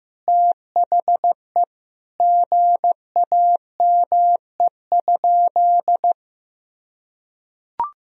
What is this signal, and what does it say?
Morse code, heard as a keyed tone, about 15 wpm, 700 Hz. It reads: THE GAME?
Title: Sentences from Top 1000 Words - Review Part 4 - 15wpm